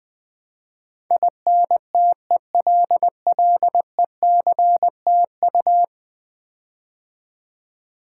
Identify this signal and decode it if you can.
Morse code, heard as a keyed tone, about 19 wpm, 700 Hz. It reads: INTELLECTU
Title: Build A Word - 5 Letters At A Time - 12-Letter Words - 20wpm